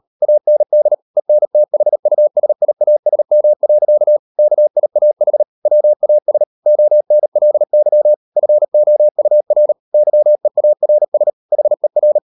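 Morse code, heard as a keyed tone, about 38 wpm, 600 Hz. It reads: AND ENTHUSIASM. KIAH WAS ONLY FOUR YEARS HER
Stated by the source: Morse code practice at 38 WPM, sending Ken Follett's novel Never in English, from CW med Gnister.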